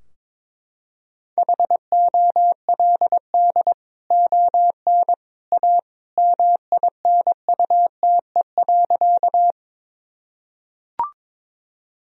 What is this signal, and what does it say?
Morse code, heard as a keyed tone, about 22 wpm, 700 Hz. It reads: HOLD ON A MINUTE.
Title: Sentences from Top 1000 Words - Review Part 2 - 22wpm